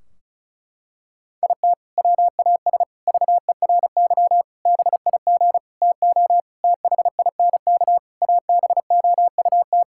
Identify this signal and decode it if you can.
Morse code, heard as a keyed tone, about 36 wpm, 700 Hz. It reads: IT WAS VERY BIG TO THINK ABOUT
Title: Sentences from Top 1600 Words 35wpm